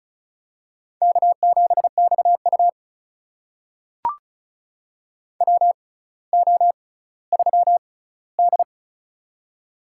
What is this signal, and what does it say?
Morse code, heard as a keyed tone, about 33 wpm, spaced as for 18 wpm, 700 Hz. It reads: K7XU WO3D
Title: USA Call Signs - 6x Character Spacing - 35wpm